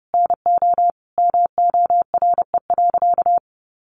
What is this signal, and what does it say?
Morse code, heard as a keyed tone, about 30 wpm, 700 Hz. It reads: NO MORE.
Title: Sentences from Top 800 Words 30wpm